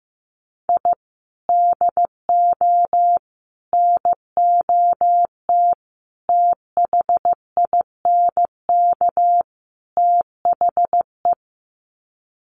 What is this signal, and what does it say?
Morse code, heard as a keyed tone, about 15 wpm, 700 Hz. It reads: I DO NOT THINK THE
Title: Sets of 5 Words 15wpm